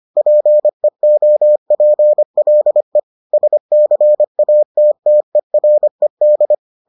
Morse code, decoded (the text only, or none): PEOPLE SCATTERED